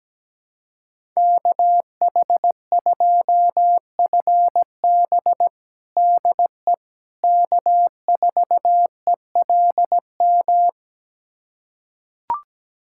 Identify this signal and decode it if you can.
Morse code, heard as a keyed tone, about 17 wpm, 700 Hz. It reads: KH2FB DE K4ELM
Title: USA Callsign DE Callsign - 5 Characters - 17wpm